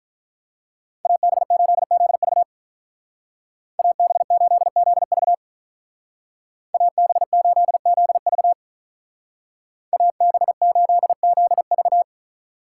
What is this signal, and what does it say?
Morse code, heard as a keyed tone, about 47 wpm, 700 Hz. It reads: AB8ZV AB8ZV AB8ZV AB8ZV